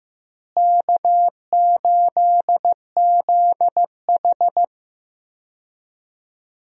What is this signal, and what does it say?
Morse code, heard as a keyed tone, about 15 wpm, 700 Hz. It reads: K8ZH
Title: USA Call Signs - 4 Characters - Repeated 3x - 15wpm